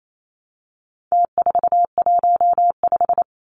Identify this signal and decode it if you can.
Morse code, heard as a keyed tone, about 28 wpm, 700 Hz. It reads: T415
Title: Numbers - Contest Style - 4-Digit 28wpm